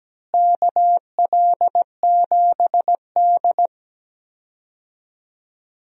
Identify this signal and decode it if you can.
Morse code, heard as a keyed tone, about 17 wpm, 700 Hz. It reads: KL7D
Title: DXCC-100 Call Signs - 4 Characters - Repeated 3x - 17wpm